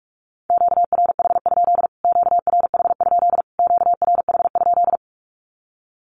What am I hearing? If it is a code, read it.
Morse code, heard as a keyed tone, about 44 wpm, 700 Hz. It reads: QRH? QRH? QRH?
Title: Q-Codes Repeated 3x 45wpm